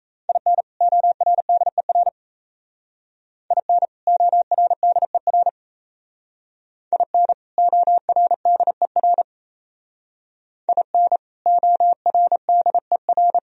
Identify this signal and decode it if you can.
Morse code, heard as a keyed tone, about 36 wpm, 700 Hz. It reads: IN ORDER IN ORDER IN ORDER IN ORDER